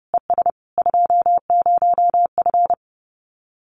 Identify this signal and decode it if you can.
Morse code, heard as a keyed tone, about 30 wpm, 700 Hz. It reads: ES 20F
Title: Common QSO Phrases Encore 30wpm